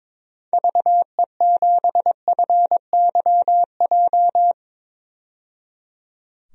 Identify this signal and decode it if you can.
Morse code, heard as a keyed tone, about 22 wpm, 700 Hz. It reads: VE7FYJ
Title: DXCC-100 Call Signs - 6 Characters - 22wpm